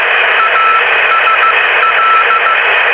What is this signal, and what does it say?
Morse code, heard as a keyed tone, about 15 wpm, 1310 Hz, with radio noise. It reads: ASL